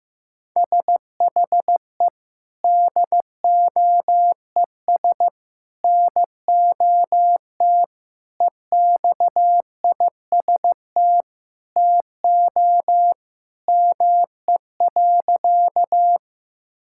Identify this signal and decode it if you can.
Morse code, heard as a keyed tone, about 15 wpm, 700 Hz. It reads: SHE DOES NOT EXIST TO ME.
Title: Sentences from Top 1000 Words - Review Part 5 - 15wpm